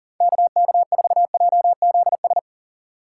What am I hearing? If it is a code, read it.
Morse code, heard as a keyed tone, about 50 wpm, 700 Hz. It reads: KK4JZS
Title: Speed Racing - USA 2x3 Call Signs 40wpm